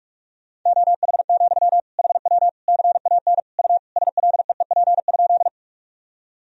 Morse code, heard as a keyed tone, about 46 wpm, 700 Hz. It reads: OH, HW CAN U SLEEP?
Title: Sentences from Top 500 Words and Abbreviations 45wpm